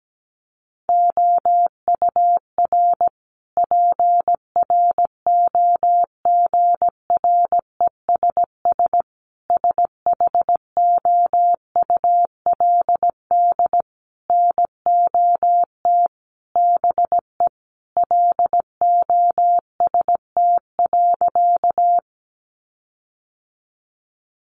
Morse code, heard as a keyed tone, about 17 wpm, 700 Hz. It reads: OUR PROGRESS SHOULD NOT BE LOST.
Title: Sentences from Top 1400 Words 17wpm